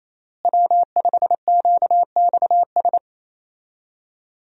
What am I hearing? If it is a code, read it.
Morse code, heard as a keyed tone, about 28 wpm, 700 Hz. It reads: W5QXS
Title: USA Call Signs - 5 Characters - Repeated 3x - 28wpm